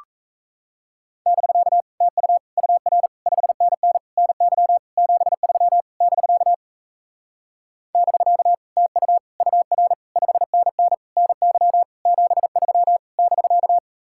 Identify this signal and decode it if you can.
Morse code, heard as a keyed tone, about 37 wpm, 700 Hz. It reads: <BK> TU UR 5NN NY 73 <BK> <BK> TU UR 5NN NY 73 <BK>